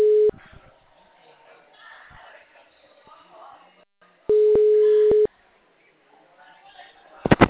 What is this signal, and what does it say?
Ringback tone (Europe, 425 Hz), with speech.